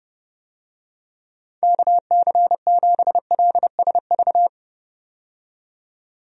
Morse code, heard as a keyed tone, about 30 wpm, 700 Hz. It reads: KC7LSV
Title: USA Call Signs - 6 Characters - 30wpm